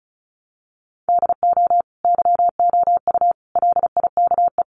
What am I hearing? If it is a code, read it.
Morse code, heard as a keyed tone, about 34 wpm, 700 Hz. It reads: DO YOU LIKE
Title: Sentences from Top 100 Words 35wpm